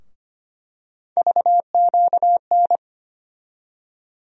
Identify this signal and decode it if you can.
Morse code, heard as a keyed tone, about 25 wpm, 700 Hz. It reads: VQN
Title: USA Call Sign Postfixes - 3 Letters 25wpm